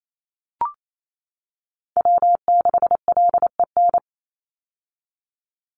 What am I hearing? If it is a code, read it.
Morse code, heard as a keyed tone, about 28 wpm, 700 Hz. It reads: W6LEN
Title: Call Signs - 28wpm